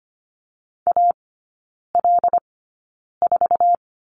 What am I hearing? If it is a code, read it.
Morse code, heard as a keyed tone, about 25 wpm, spaced as for 9 wpm, 690 Hz. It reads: AL4
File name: Lesson 15e - Number 5 - Three Characters 25wpm